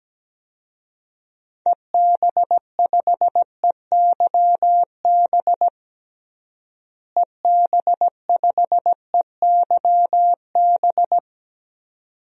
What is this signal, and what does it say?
Morse code, heard as a keyed tone, about 17 wpm, 700 Hz. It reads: EB5EYB EB5EYB
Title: DXCC-100 Call Signs - 6 Characters - Repeated 2x - 17wpm